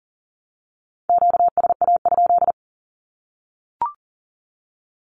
Morse code, heard as a keyed tone, about 38 wpm, 700 Hz. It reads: QSA?
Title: Q-Codes 40wpm